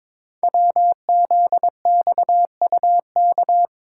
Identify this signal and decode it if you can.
Morse code, heard as a keyed tone, about 23 wpm, 700 Hz. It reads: WZXUK
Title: Code Groups - 5 Letter:Numbers - 22wpm